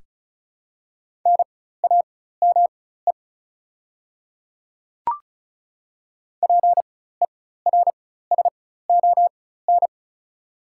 Morse code, heard as a keyed tone, about 36 wpm, spaced as for 16 wpm, 690 Hz. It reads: NAME PERSON
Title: 4x Character Spacing - Top 500 Words - 35wpm